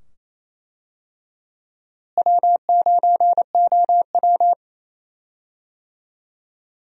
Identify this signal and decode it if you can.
Morse code, heard as a keyed tone, about 27 wpm, 700 Hz. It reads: W9OW
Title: USA Call Signs - 4 Characters - Repeated 2x - 28wpm